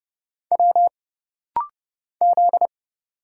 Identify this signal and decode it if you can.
Morse code, heard as a keyed tone, about 31 wpm, 700 Hz. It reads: W Z
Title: Single Letter-Number Alt - Rapid-Fire 30wpm